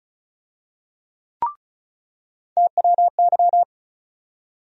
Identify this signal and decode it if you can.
Morse code, heard as a keyed tone, about 34 wpm, 700 Hz. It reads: TWY
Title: USA Call Sign Postfixes - 3 Letters 35wpm